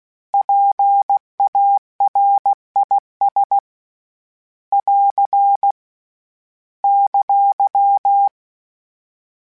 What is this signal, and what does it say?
Morse code, heard as a keyed tone, about 16 wpm, 800 Hz. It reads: PARIS <AR> !
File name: PARIS Calibration / PARIS mp3s / PARIS@15